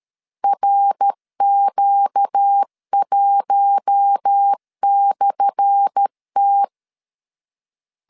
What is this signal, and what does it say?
Morse code, heard as a keyed tone, about 13 wpm, 800 Hz. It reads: RQ1/T
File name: LEZIONI / 10 SIGNS / 10A SLASH 12wpm 2314